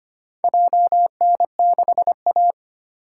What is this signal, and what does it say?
Morse code, heard as a keyed tone, about 25 wpm, 700 Hz. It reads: JN6A
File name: practice / 25wpm / Four Letter-Number Combinations - 25wpm